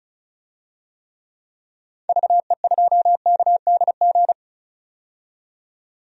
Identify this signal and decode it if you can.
Morse code, heard as a keyed tone, about 34 wpm, 700 Hz. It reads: VE2KDG
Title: DXCC-100 Call Signs - 6 Characters - 35wpm